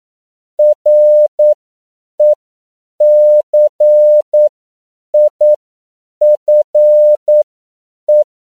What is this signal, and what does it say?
Morse code, heard as a keyed tone, about 9 wpm, spaced as for 7 wpm, 600 Hz. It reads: RECIFE
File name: CW / 10PPM